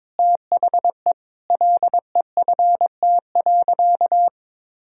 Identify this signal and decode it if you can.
Morse code, heard as a keyed tone, about 22 wpm, 700 Hz. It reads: THE LEFT.
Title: Sentences from Top 1000 Words - Review Part 1 - 22wpm